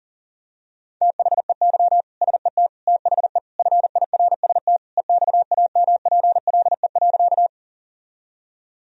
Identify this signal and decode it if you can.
Morse code, heard as a keyed tone, about 40 wpm, 700 Hz. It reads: THEY SET THE FIRST EXAMPLE.
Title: Sentences from Top 1000 Words - Review Part 1 - 40wpm